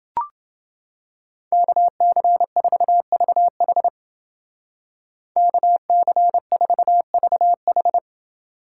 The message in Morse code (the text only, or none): KC4VH KC4VH